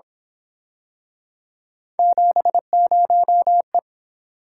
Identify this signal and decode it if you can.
Morse code, heard as a keyed tone, about 27 wpm, 700 Hz. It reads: 70E
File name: Speed-Racing - Three Letter-Number Combinations - 22wpm